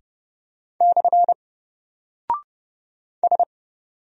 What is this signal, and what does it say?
Morse code, heard as a keyed tone, about 30 wpm, 700 Hz. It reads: / S
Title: Single Character (Letter:Number:Punctuation:Prosign) Alt 30wpm